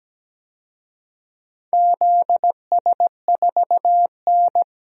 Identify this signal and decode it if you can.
Morse code, heard as a keyed tone, about 17 wpm, 700 Hz. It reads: ZS4N
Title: DXCC-100 Call Signs - 4 Characters - Repeated 3x - 17wpm